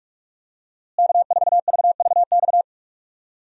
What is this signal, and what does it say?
Morse code, heard as a keyed tone, about 46 wpm, 700 Hz. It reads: K4VVX
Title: USA Call Signs - 5 Characters - Repeated 3x - 45wpm